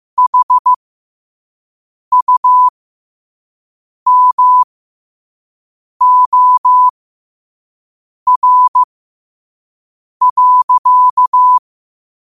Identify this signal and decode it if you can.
Morse code, heard as a keyed tone, about 15 wpm, spaced as for 5 wpm, 1000 Hz. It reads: HUMOR.
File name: qotdfiles / 5wpm